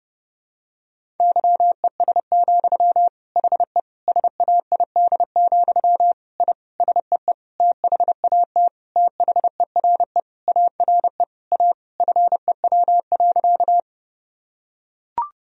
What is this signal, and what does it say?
Morse code, heard as a keyed tone, about 30 wpm, 700 Hz. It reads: YES, HE SAID, I SEE THAT THERE ARE A FEW.
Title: Sentences from Top 1000 Words - Review Part 1 - 30wpm